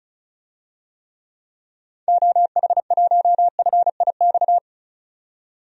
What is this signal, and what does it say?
Morse code, heard as a keyed tone, about 36 wpm, 700 Hz. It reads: OH1FIX
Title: DXCC-100 Call Signs - 6 Characters - 35wpm